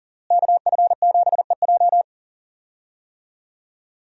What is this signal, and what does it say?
Morse code, heard as a keyed tone, about 40 wpm, 700 Hz. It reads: KF7EJ
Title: USA Call Signs - 5 Characters - Repeated 3x - 40wpm